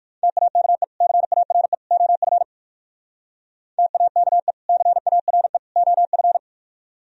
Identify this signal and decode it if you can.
Morse code, heard as a keyed tone, about 46 wpm, 700 Hz. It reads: TAKE CARE OF TAKE CARE OF
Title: Speed-Racing - Sets of 3 Words 45wpm